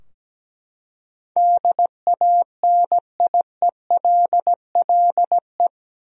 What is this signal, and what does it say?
Morse code, heard as a keyed tone, about 17 wpm, 700 Hz. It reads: DANIELLE
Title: USA Common Names - 17wpm